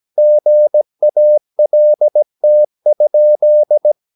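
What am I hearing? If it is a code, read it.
Morse code, heard as a keyed tone, about 17 wpm, 600 Hz. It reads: GALT?
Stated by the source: Slow (QRS) Morse practice at 17 wpm, from CW med Gnister.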